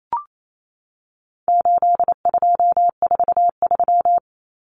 Morse code, heard as a keyed tone, about 28 wpm, 700 Hz. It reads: 8243
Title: Common Cut Numbers - 4-Digit 28wpm